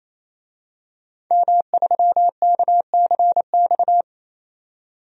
Morse code, heard as a keyed tone, about 28 wpm, 700 Hz. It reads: M3KCX
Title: DXCC-100 Call Signs - 5 Characters - 28wpm